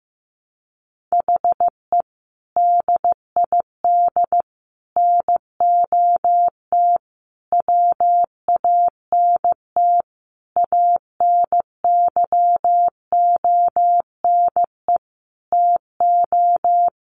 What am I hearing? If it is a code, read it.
Morse code, heard as a keyed tone, about 15 wpm, 700 Hz. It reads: HE DID NOT WANT ANYONE TO
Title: Sentences from Top 1100 Words 15wpm